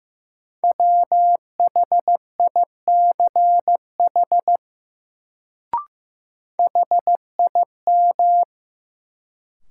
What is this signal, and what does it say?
Morse code, heard as a keyed tone, about 15 wpm, 700 Hz. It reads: WHICH HIM
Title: Sentences from Top 100 Words - New Word Drill - 15wpm